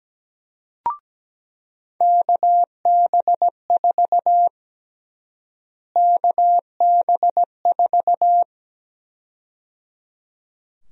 Morse code, heard as a keyed tone, about 17 wpm, 700 Hz. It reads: KB4 KB4
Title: USA Call Sign Prefixes - Repeated 2x - 17wpm